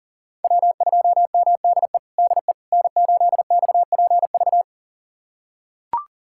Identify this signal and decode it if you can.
Morse code, heard as a keyed tone, about 40 wpm, 700 Hz. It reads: W2MDE DE N8XPV
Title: USA Callsign DE Callsign - 5 Characters - 40wpm